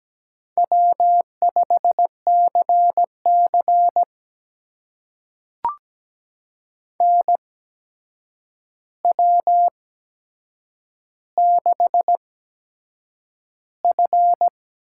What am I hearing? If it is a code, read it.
Morse code, heard as a keyed tone, about 17 wpm, spaced as for 7 wpm, 700 Hz. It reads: W5CC NW6F
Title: USA Call Signs - 4 Characters - 8x Character Spacing - 17wpm